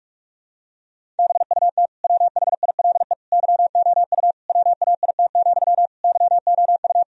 Morse code, heard as a keyed tone, about 42 wpm, 700 Hz. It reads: BUT WHILE YOU WAIT, YOU
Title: Sentences from Top 1000 Words - Review Part 1 - 45wpm